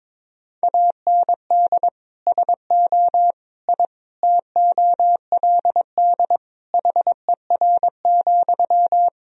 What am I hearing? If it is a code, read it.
Morse code, heard as a keyed tone, about 22 wpm, 700 Hz. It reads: AND SO I TOLD HER,
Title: Sentences from Top 1000 Words - Review Part 4 - 22wpm